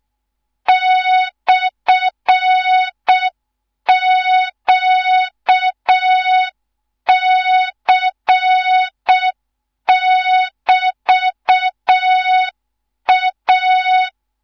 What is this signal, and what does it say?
Morse code, heard as a keyed tone, about 6 wpm, 3700 Hz. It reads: /QC<BT>A
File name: Morze-HA2MN / hang / 14 280 HA2MN